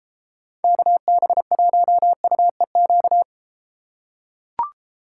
Morse code, heard as a keyed tone, about 31 wpm, 700 Hz. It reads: KB1UEQ